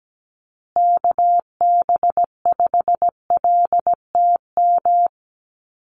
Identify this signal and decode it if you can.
Morse code, heard as a keyed tone, about 17 wpm, 700 Hz. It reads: KB5LTM
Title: USA Call Signs Repeated 2x 17wpm